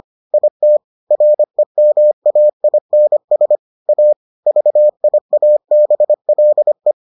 Morse code, heard as a keyed tone, about 25 wpm, 600 Hz. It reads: IT REMAINS A VIABLE